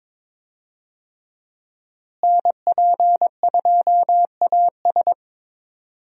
Morse code, heard as a keyed tone, about 21 wpm, 700 Hz. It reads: NP2AS
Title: USA Call Signs - 5 Characters - Repeated 3x - 22wpm